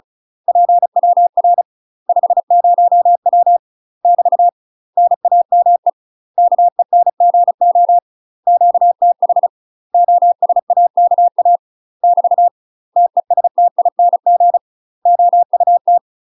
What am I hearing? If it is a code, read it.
Morse code, heard as a keyed tone, about 36 wpm, 700 Hz. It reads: PWR 50W <BT> NAME KENGO QTH OSAKA <BT> TESTING OUT
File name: UK MASTER QSOs TRAINING / 35WPM / UK MASTER QSOs 0036